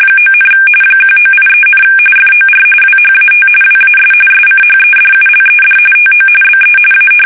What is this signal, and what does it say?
RTTY, about 45.45 baud. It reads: 9, 1983, FOR WHAT WAS SCHEDULED TO BE A
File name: fsk850 / battleshipnj